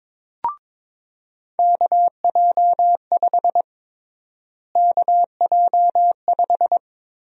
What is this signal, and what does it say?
Morse code, heard as a keyed tone, about 22 wpm, 700 Hz. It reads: KJ5 KJ5
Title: USA Call Sign Prefixes - Repeated 2x - 22wpm